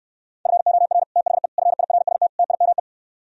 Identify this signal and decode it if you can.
Morse code, heard as a keyed tone, about 69 wpm, 700 Hz. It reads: FOR THE FIRST TIME